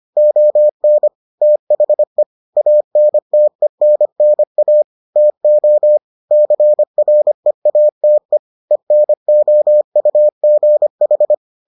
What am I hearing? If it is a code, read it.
Morse code, heard as a keyed tone, about 25 wpm, 600 Hz. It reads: ON THE ANTENNA TO CREATE ENOUGH